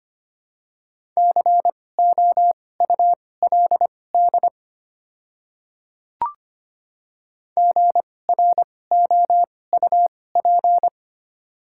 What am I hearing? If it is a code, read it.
Morse code, heard as a keyed tone, about 25 wpm, spaced as for 18 wpm, 700 Hz. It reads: COULD GROUP
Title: 2x Character Spacing - Top 300 Words - 25wpm